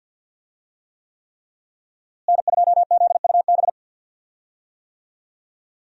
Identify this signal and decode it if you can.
Morse code, heard as a keyed tone, about 52 wpm, 700 Hz. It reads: N2ZUB
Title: USA Call Signs - 5 Characters - Repeated 3x - 50wpm